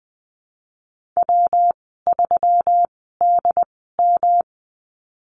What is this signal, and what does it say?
Morse code, heard as a keyed tone, about 20 wpm, spaced as for 15 wpm, 700 Hz. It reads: W3DM
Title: 161 Lesson 26g - Letter g - Call Signs 20wpm